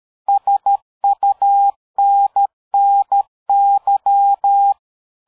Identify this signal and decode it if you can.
Morse code, heard as a keyed tone, about 13 wpm, 800 Hz. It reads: SUNNY
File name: dk5ke / QSO 60BpM